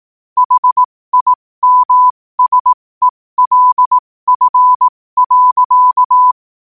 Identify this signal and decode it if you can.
Morse code, heard as a keyed tone, about 18 wpm, spaced as for 15 wpm, 1000 Hz. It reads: HIMSELF.
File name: qotdfiles / 15wpm